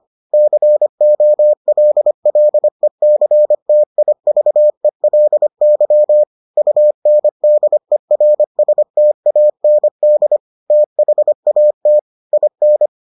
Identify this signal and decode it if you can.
Morse code, heard as a keyed tone, about 25 wpm, 600 Hz. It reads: COLLECTIVELY UNDERSTAND THAT IN